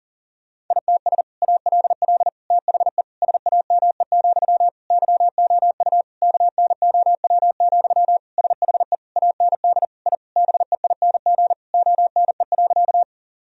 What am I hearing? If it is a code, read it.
Morse code, heard as a keyed tone, about 37 wpm, 700 Hz. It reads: ITS ALL THE SAME, YOU KNOW, SHE AND I BEING ONE.
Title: Sentences from Top 1000 Words - Review Part 3 - 40wpm